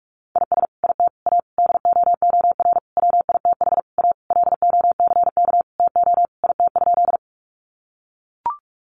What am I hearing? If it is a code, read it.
Morse code, heard as a keyed tone, about 44 wpm, 700 Hz. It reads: IS IT A DOOR WITH A LOCK TO IT?
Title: Sentences from Top 1300 Words 45wpm